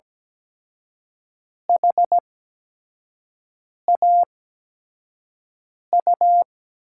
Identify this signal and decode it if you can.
Morse code, heard as a keyed tone, about 17 wpm, spaced as for 5 wpm, 700 Hz. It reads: HAU
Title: USA Call Signs - 8x Character Spacing - 17wpm